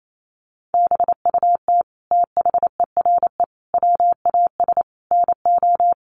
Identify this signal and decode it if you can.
Morse code, heard as a keyed tone, about 28 wpm, 700 Hz. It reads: BUT THERE WAS NO
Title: Sentences from Top 1500 Words 28wpm